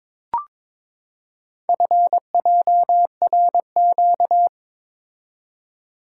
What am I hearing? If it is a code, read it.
Morse code, heard as a keyed tone, about 22 wpm, 690 Hz. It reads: FJRQ
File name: practice / 22wpm / Four Letter-Number Combinations - 22wpm